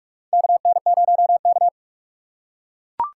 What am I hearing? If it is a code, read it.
Morse code, heard as a keyed tone, about 43 wpm, 700 Hz. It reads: KN0K
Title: USA Call Signs - 4x Character Spacing - 45wpm